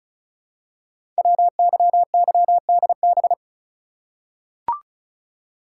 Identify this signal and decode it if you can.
Morse code, heard as a keyed tone, about 36 wpm, 700 Hz. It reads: WYYDB
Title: Code Groups - 5 Letters - 35wpm